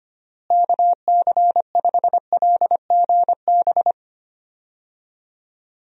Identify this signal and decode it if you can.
Morse code, heard as a keyed tone, about 25 wpm, 700 Hz. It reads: KC5LGB
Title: USA Call Signs - 6 Characters - Repeated 3x - 25wpm